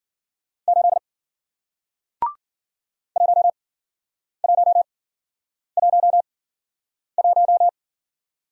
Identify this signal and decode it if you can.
Morse code, heard as a keyed tone, about 53 wpm, spaced as for 9 wpm, 700 Hz. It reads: 8 1111